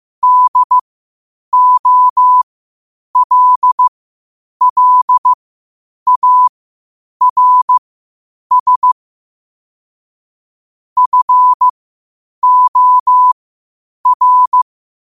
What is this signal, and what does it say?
Morse code, heard as a keyed tone, about 15 wpm, spaced as for 8 wpm, 1000 Hz. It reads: DOLLARS FOR
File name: qotdfiles / 8wpm